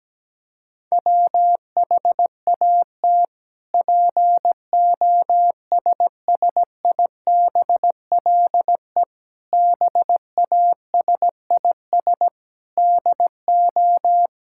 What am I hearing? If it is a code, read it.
Morse code, heard as a keyed tone, about 17 wpm, 700 Hz. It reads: WHAT POSSIBLE BASIS DO
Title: Sentences from Top 1000 Words - Review Part 5 - 17wpm